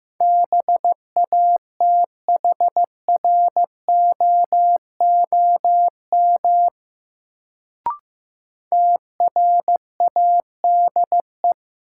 Morse code, heard as a keyed tone, about 15 wpm, 700 Hz. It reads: BATHROOM TRADE
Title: Sets of 5 Words Encore - New Word Drill - 15wpm